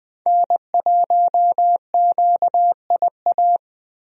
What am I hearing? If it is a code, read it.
Morse code, heard as a keyed tone, about 20 wpm, 700 Hz. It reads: N1QIA
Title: USA Call Signs - 5 Characters - Repeated 3x - 20wpm